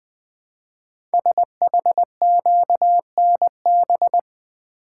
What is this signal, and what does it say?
Morse code, heard as a keyed tone, about 20 wpm, 700 Hz. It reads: SHQNB
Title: Code Groups - 5 Letters - 20wpm